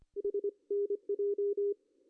Morse code, heard as a keyed tone, about 24 wpm, 410 Hz. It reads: HNJ
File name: CW / cqww-cw-k5tr